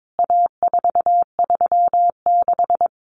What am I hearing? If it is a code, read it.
Morse code, heard as a keyed tone, about 22 wpm, 700 Hz. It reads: A436
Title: CWT Exchange 22wpm